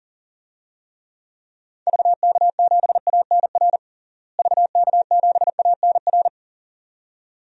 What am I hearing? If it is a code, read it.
Morse code, heard as a keyed tone, about 37 wpm, 700 Hz. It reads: VK7ANR VK7ANR
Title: DXCC-100 Call Signs - 6 Characters - Repeated 2x - 40wpm